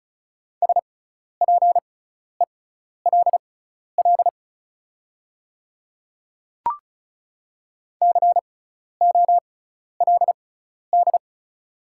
Morse code, heard as a keyed tone, about 36 wpm, spaced as for 12 wpm, 690 Hz. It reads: SPELL COLD
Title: 6x Character Spacing - Top 500 Words - 35wpm